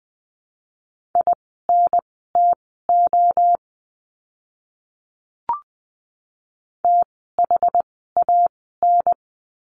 Morse code, heard as a keyed tone, about 20 wpm, spaced as for 14 wpm, 700 Hz. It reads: INTO THAN
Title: 185 Lesson 30f - Letter b - Words 20wpm